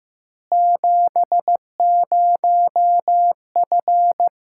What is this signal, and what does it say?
Morse code, heard as a keyed tone, about 15 wpm, 700 Hz. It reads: 70F
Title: K1USN SST QSO Practice 15wpm